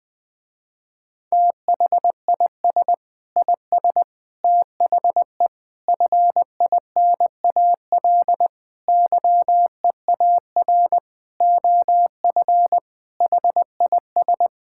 Morse code, heard as a keyed tone, about 20 wpm, 700 Hz. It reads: THIS IS THE FINAL YEAR OF HIS
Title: Sentences with 6-Word Phrases - Spoken Beginning - 20wpm